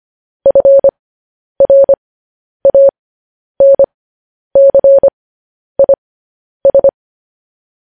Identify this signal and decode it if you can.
Morse code, heard as a keyed tone, about 25 wpm, spaced as for 10 wpm, 550 Hz. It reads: FRANCIS